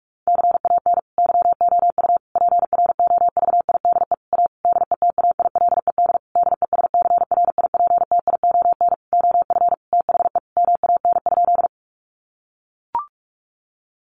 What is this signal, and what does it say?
Morse code, heard as a keyed tone, about 44 wpm, 700 Hz. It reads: CAN YOU PROVIDE A DETAILED DESCRIPTION OF THE MAN?
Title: Sentences from Top 1500 Words 45wpm